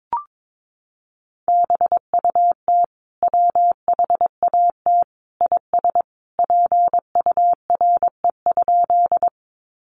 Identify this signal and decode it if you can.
Morse code, heard as a keyed tone, about 22 wpm, 700 Hz. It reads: BUT WHAT IS PURE?
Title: Sentences from Top 1500 Words 22wpm